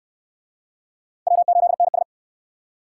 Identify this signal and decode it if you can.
Morse code, heard as a keyed tone, about 63 wpm, 700 Hz. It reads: W8NS